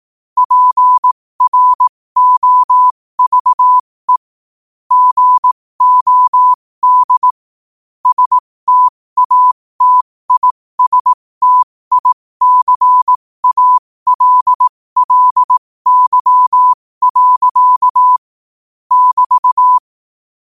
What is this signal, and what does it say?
Morse code, heard as a keyed tone, about 18 wpm, spaced as for 15 wpm, 1000 Hz. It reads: PROVE GOD STATISTICALLY. <BT>
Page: 15 wpm morse code quotes